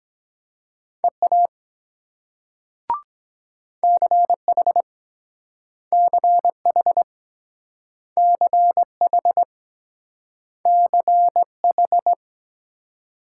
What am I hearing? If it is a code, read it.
Morse code, heard as a keyed tone, about 22 wpm, 700 Hz. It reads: EA CH CH CH CH